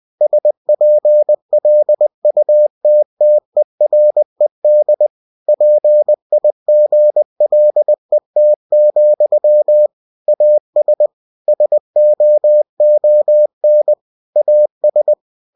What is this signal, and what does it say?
Morse code, heard as a keyed tone, about 20 wpm, 600 Hz. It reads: SPLUTTERED PIGLET, AS SOON AS
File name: cw-books / winnie-the-pooh / winnie-20wpm-page-0072